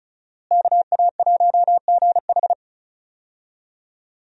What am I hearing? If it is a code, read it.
Morse code, heard as a keyed tone, about 36 wpm, 700 Hz. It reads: KA1GH
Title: USA Call Signs - 5 Characters - Repeated 3x - 35wpm